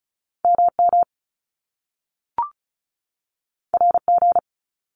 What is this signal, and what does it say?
Morse code, heard as a keyed tone, about 34 wpm, 700 Hz. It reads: MM RG
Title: Top Two Letter Combinations in Words 35wpm